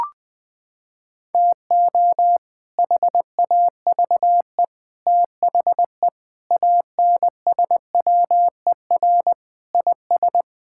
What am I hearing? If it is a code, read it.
Morse code, heard as a keyed tone, about 20 wpm, 700 Hz. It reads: TO HAVE THE ANSWER IS
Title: Sentences with 3-Word Phrases - 20wpm